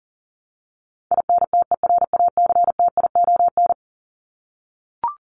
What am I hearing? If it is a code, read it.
Morse code, heard as a keyed tone, about 40 wpm, 700 Hz. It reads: INTERACTION
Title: Top 1251-1500 Words 40wpm